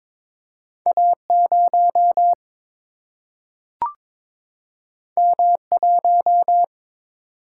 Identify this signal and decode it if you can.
Morse code, heard as a keyed tone, about 22 wpm, 700 Hz. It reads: A0 M1
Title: Single-Letter Single-Number 22wpm